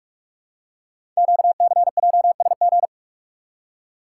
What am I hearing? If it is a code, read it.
Morse code, heard as a keyed tone, about 44 wpm, 700 Hz. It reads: QCJSG
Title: Code Groups - 5 Letters - 45wpm